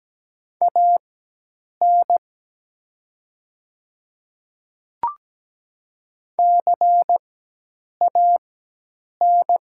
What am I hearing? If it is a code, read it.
Morse code, heard as a keyed tone, about 17 wpm, spaced as for 8 wpm, 700 Hz. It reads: AN CAN
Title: 4x Character Spacing - Top 100 Words - 17wpm